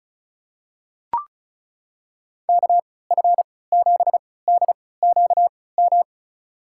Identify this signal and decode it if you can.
Morse code, heard as a keyed tone, about 36 wpm, spaced as for 20 wpm, 700 Hz. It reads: KF7DQM
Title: USA Call Signs - 6 Characters - 3x Character Spacing - 35wpm